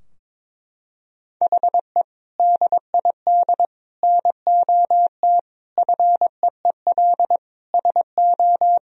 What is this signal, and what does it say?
Morse code, heard as a keyed tone, about 22 wpm, 700 Hz. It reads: HE DID NOT FEEL SO
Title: Sentences from Top 1000 Words - Review Part 4 - 22wpm